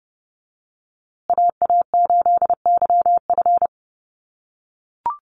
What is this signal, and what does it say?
Morse code, heard as a keyed tone, about 30 wpm, 700 Hz. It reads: AA8YF
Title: USA Call Signs Repeated 3x 30wpm